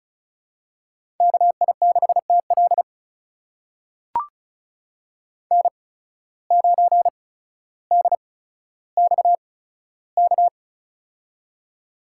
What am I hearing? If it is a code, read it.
Morse code, heard as a keyed tone, about 36 wpm, spaced as for 15 wpm, 700 Hz. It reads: KI6TL N9DXK
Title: USA Call Signs - 5 Characters - 8x Character Spacing - 35wpm